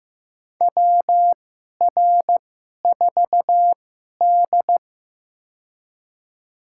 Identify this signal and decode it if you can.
Morse code, heard as a keyed tone, about 15 wpm, spaced as for 11 wpm, 700 Hz. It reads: WR4D
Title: USA Call Signs - 4 Characters - 2x Character Spacing - 15wpm